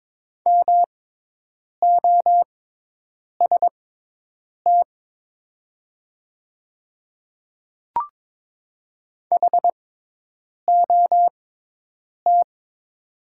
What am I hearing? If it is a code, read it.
Morse code, heard as a keyed tone, about 22 wpm, spaced as for 8 wpm, 700 Hz. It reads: MOST HOT
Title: 6x Character Spacing - Top 200 Words - 22wpm